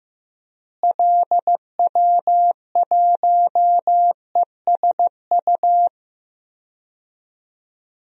Morse code, heard as a keyed tone, about 16 wpm, 700 Hz. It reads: LW1ESU
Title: DXCC-100 Call Signs - 6 Characters - Repeated 3x - 15wpm